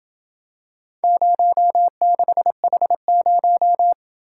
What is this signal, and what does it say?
Morse code, heard as a keyed tone, about 27 wpm, 700 Hz. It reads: 06H0